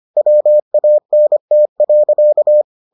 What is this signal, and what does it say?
Morse code, heard as a keyed tone, about 25 wpm, 600 Hz. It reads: WANT.